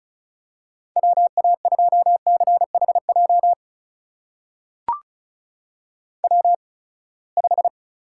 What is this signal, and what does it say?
Morse code, heard as a keyed tone, about 36 wpm, spaced as for 24 wpm, 700 Hz. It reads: WA2CHJ W5
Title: USA Call Signs - 8x Character Spacing - 35wpm